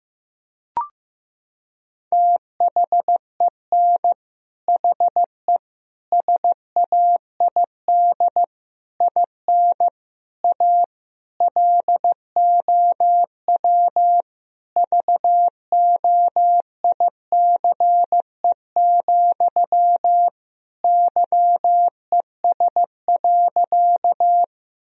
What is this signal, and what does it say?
Morse code, heard as a keyed tone, about 15 wpm, 700 Hz. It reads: THEN HE SAID IN A LOW VOICE, YES.
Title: Sentences from Top 1000 Words - Review Part 1 - 15wpm